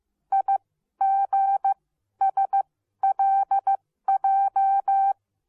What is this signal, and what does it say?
Morse code, heard as a keyed tone, about 15 wpm, spaced as for 12 wpm, 780 Hz. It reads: IGSLJ